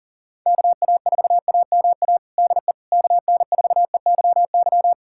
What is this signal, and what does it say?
Morse code, heard as a keyed tone, about 40 wpm, 700 Hz. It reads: KA4AMA DE KN4EYY
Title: USA Callsign DE Callsign 40wpm